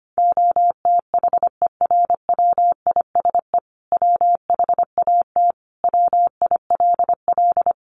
Morse code, heard as a keyed tone, about 25 wpm, 700 Hz. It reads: OTHERWISE WHAT WILL
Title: Sentences from Top 1200 Words 25wpm